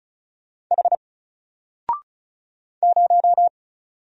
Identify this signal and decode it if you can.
Morse code, heard as a keyed tone, about 35 wpm, 700 Hz. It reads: H 0
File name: Single-Letter-Number-35wpm